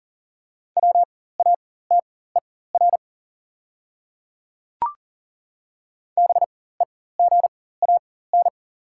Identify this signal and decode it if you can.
Morse code, heard as a keyed tone, about 37 wpm, spaced as for 18 wpm, 700 Hz. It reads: WATER BEGAN
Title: 4x Character Spacing - Top 500 Words - 40wpm